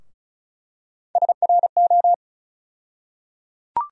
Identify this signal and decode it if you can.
Morse code, heard as a keyed tone, about 34 wpm, 700 Hz. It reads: SRO
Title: USA Call Sign Postfixes - 3 Letters 35wpm